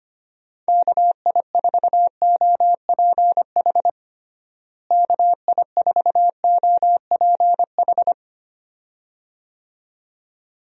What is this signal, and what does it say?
Morse code, heard as a keyed tone, about 25 wpm, 700 Hz. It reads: KI4OPH KI4OPH
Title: USA Call Signs - 6 Characters - Repeated 2x - 25wpm